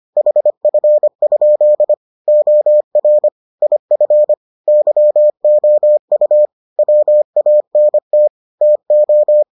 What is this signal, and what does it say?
Morse code, heard as a keyed tone, about 25 wpm, 600 Hz. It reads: HF? OR IF YOU WANT TO